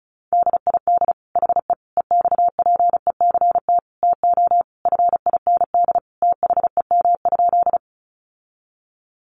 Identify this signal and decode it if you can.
Morse code, heard as a keyed tone, about 34 wpm, 700 Hz. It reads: DID HE EXPECT TO FIND THEM?
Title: Sentences from Top 1100 Words 35wpm